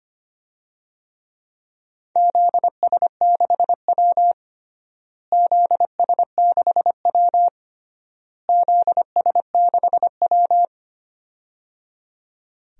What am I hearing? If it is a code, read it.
Morse code, heard as a keyed tone, about 25 wpm, 700 Hz. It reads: ZS6W ZS6W ZS6W
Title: DXCC-100 Call Signs - 4 Characters - Repeated 3x - 25wpm